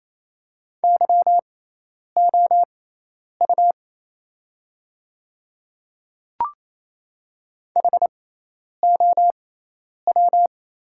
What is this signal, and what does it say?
Morse code, heard as a keyed tone, about 27 wpm, spaced as for 10 wpm, 700 Hz. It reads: YOU HOW
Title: 6x Character Spacing - Top 100 Words - 28wpm